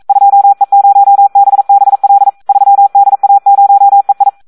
Morse code, heard as a keyed tone, about 42 wpm, 800 Hz. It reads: 2E0BBL 3DA0EI
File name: Audio practica CW / SD 1 42